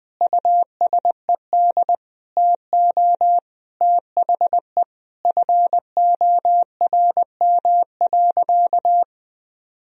Morse code, heard as a keyed tone, about 20 wpm, 700 Hz. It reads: USED TO THE FORM.
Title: Sentences from Top 1000 Words - Review Part 1 - 20wpm